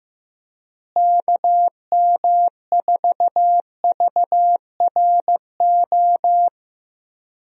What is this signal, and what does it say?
Morse code, heard as a keyed tone, about 15 wpm, 700 Hz. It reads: KM4VRO
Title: USA Call Signs - 6 Characters - Repeated 3x - 15wpm